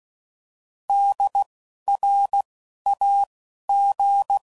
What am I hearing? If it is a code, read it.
Morse code, heard as a keyed tone, about 16 wpm, spaced as for 11 wpm, 800 Hz. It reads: DRAG
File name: Prefixes / By First Letter / Ds